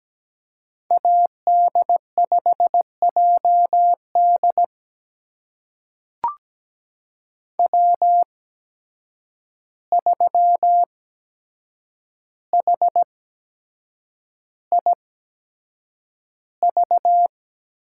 Morse code, heard as a keyed tone, about 17 wpm, spaced as for 7 wpm, 700 Hz. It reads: AD5JD W3HIV
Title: USA Call Signs - 5 Characters - 8x Character Spacing - 17wpm